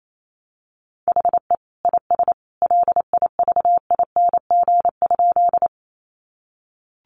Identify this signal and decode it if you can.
Morse code, heard as a keyed tone, about 28 wpm, 700 Hz. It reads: HE IS LIVING?
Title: Sentences from Top 1100 Words 28wpm